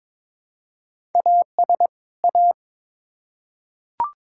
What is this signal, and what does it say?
Morse code, heard as a keyed tone, about 22 wpm, 700 Hz. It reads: AS A
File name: Build A Phrase - 2-Word Phrases - 22wpm